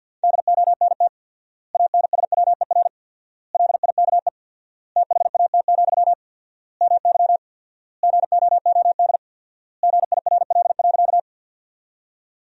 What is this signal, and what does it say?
Morse code, heard as a keyed tone, about 52 wpm, 700 Hz. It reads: DONT ANSWER LIKE THAT, MY GOOD GIRL.
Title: 4x Word-Spacing - Sentences from Top 400 Words - 50wpm